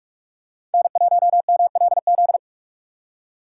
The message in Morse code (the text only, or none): N1MLZ